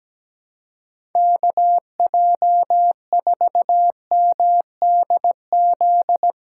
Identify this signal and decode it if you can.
Morse code, heard as a keyed tone, about 17 wpm, 700 Hz. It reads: KJ4MDZ
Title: USA Callsign DE Callsign - 6 Characters - 17wpm